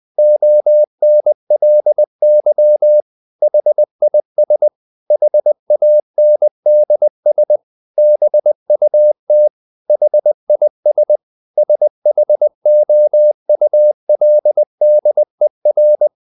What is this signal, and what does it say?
Morse code, heard as a keyed tone, about 20 wpm, 600 Hz. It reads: ONLY HIS HANDS BUT HIS SHOULDER
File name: the old man and the sea part 2 20wpm